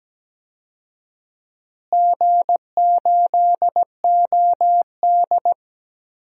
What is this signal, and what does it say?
Morse code, heard as a keyed tone, about 17 wpm, 700 Hz. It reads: G8OD
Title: DXCC-100 Call Signs - 4 Characters - Repeated 3x - 17wpm